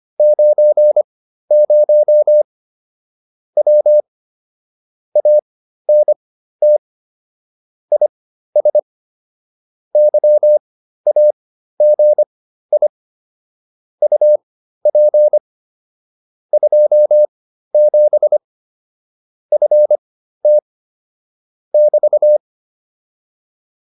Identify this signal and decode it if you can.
Morse code, heard as a keyed tone, about 25 wpm, spaced as for 13 wpm, 600 Hz. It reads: 90 W ANT IS YAGI UP 27 FT <BT>
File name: QSO307 13